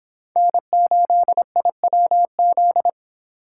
Morse code, heard as a keyed tone, about 24 wpm, 700 Hz. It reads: N8IWZ